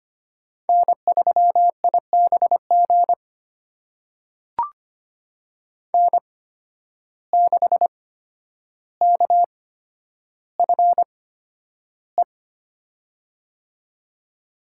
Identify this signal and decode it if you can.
Morse code, heard as a keyed tone, about 25 wpm, spaced as for 11 wpm, 700 Hz. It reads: N3IBG N6KFE
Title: USA Call Signs - 5 Characters - 8x Character Spacing - 25wpm